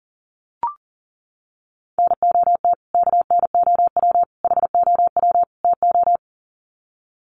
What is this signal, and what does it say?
Morse code, heard as a keyed tone, about 38 wpm, 700 Hz. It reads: NOT KNOW HOW TO
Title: Sets of 4 Words 40wpm